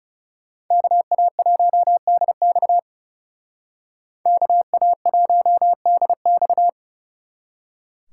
Morse code, heard as a keyed tone, about 35 wpm, 700 Hz. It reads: KA1DX KA1DX
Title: Speed Racing - USA 2x2 Call Signs 30wpm